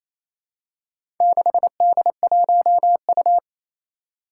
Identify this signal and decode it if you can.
Morse code, heard as a keyed tone, about 28 wpm, 700 Hz. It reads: 6D1U
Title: Four Challenging Letter-Number Combinations 28wpm